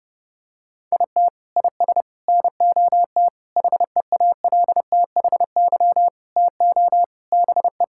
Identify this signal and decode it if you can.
Morse code, heard as a keyed tone, about 30 wpm, 700 Hz. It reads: IT IS NOT HEALTHY TO BE
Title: Sentences from Top 1000 Words - Review Part 6 - 30wpm